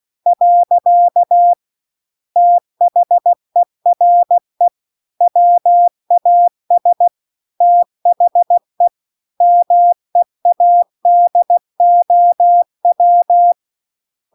Morse code, heard as a keyed tone, about 16 wpm, 700 Hz. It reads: . THERE WAS THE MEADOW